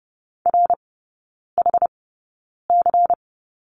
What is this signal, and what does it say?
Morse code, heard as a keyed tone, about 30 wpm, spaced as for 9 wpm, 700 Hz. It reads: RHC
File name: Lesson 18e - Letter m - Three Characters 30wpm